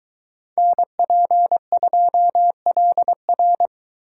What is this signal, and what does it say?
Morse code, heard as a keyed tone, about 22 wpm, 700 Hz. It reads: NP2LR